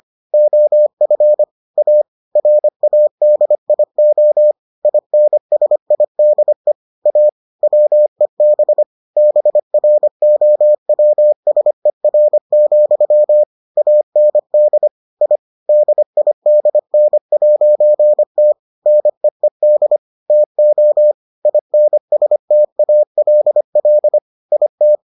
Morse code, heard as a keyed tone, about 25 wpm, 600 Hz. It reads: OF A RADIO INSIDE A WEB BROWSER, AND I DIDN'T NEED TO INSTALL IT